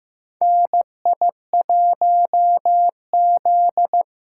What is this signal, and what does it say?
Morse code, heard as a keyed tone, about 15 wpm, 700 Hz. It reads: NI1Z
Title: USA Call Signs - 4 Characters - 6x Character Spacing - 15wpm